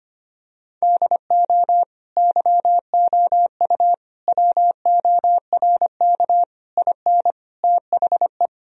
Morse code, heard as a keyed tone, about 25 wpm, 700 Hz. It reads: DO YOU WORK IN THE
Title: Sentences from Top 1000 Words - Review Part 4 - 25wpm